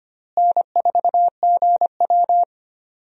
Morse code, heard as a keyed tone, about 25 wpm, 700 Hz. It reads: N4GW
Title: USA Call Signs - 4 Characters - Repeated 2x - 25wpm